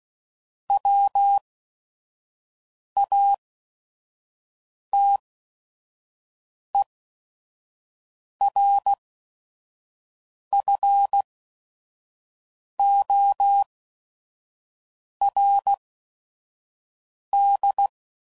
Morse code, heard as a keyed tone, about 16 wpm, spaced as for 5 wpm, 800 Hz. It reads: WATERFORD
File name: CW Code Groups @5WPM / fromSM's / Cities